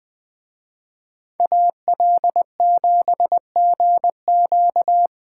Morse code, heard as a keyed tone, about 20 wpm, 700 Hz. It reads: AL7GQ
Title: Speed Racing - USA Call Signs 20wpm